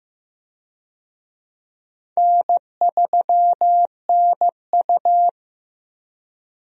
Morse code, heard as a keyed tone, about 16 wpm, 700 Hz. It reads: N3NU
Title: USA Call Signs - 4 Characters - Repeated 3x - 15wpm